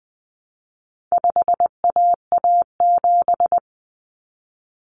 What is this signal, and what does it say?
Morse code, heard as a keyed tone, about 20 wpm, 700 Hz. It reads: 5AA7
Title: Common Cut Numbers - 4-Digit 20wpm